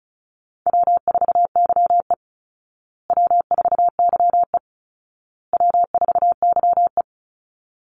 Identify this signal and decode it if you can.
Morse code, heard as a keyed tone, about 34 wpm, 700 Hz. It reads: W4YE W4YE W4YE
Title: Easy Call Signs Repeated 3x - 35wpm